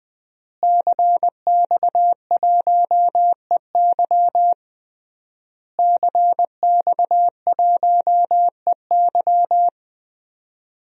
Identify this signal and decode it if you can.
Morse code, heard as a keyed tone, about 21 wpm, 700 Hz. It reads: CX1EY CX1EY
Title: DXCC-100 Call Signs - 5 Characters - Repeated 3x - 20wpm